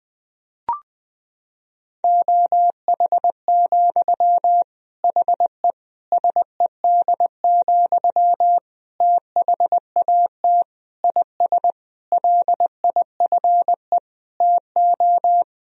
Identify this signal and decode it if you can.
Morse code, heard as a keyed tone, about 21 wpm, 700 Hz. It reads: OH, HE SED, THAT IS LIFE TO
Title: Sentences from Top 500 Words and Abbreviations 20wpm